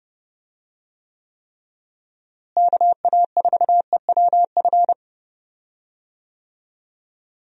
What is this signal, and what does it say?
Morse code, heard as a keyed tone, about 30 wpm, 700 Hz. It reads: KA4EWF
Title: USA Call Signs - 6 Characters - Repeated 3x - 30wpm